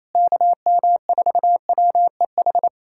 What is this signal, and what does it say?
Morse code, heard as a keyed tone, about 27 wpm, 700 Hz. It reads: KM4WEH
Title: USA Callsign DE Callsign 28wpm